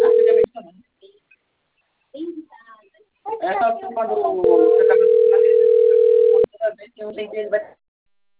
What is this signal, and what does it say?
Ringback tone (North America), 440 + 480 Hz, with speech.
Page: Campinas